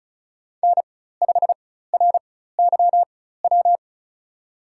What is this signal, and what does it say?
Morse code, heard as a keyed tone, about 36 wpm, spaced as for 16 wpm, 700 Hz. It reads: N5RYW
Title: USA Call Signs - 5 Characters - 4x Character Spacing - 35wpm